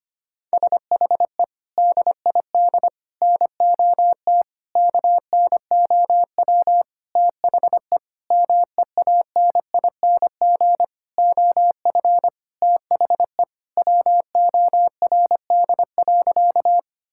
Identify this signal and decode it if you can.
Morse code, heard as a keyed tone, about 25 wpm, 700 Hz. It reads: SHE DID NOT KNOW THE MEANING OF THE WORD.
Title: Sentences from Top 1000 Words - Review Part 6 - 25wpm